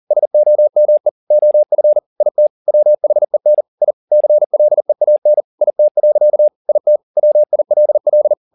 Morse code, heard as a keyed tone, about 40 wpm, 600 Hz. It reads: SOME OF IT WHEN I CLEAN IT. IT WILL
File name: the old man and the sea 2 40wpm